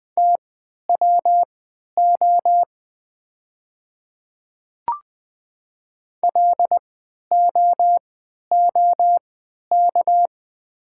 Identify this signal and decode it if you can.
Morse code, heard as a keyed tone, about 20 wpm, spaced as for 11 wpm, 700 Hz. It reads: TWO LOOK
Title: 3x Character Spacing - Top 200 Words - 20wpm